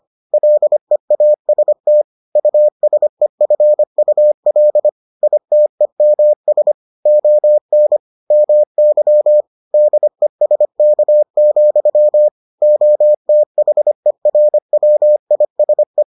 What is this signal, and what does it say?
Morse code, heard as a keyed tone, about 25 wpm, 600 Hz. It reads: LEAST USEFUL ITEMS ON MY DESK, OTHERWISE